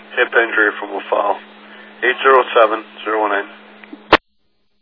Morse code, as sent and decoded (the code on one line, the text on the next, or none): none